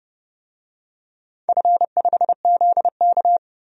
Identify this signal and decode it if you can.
Morse code, heard as a keyed tone, about 32 wpm, 700 Hz. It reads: F5ZK
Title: DXCC-100 Call Signs - 4 Characters - Repeated 2x - 30wpm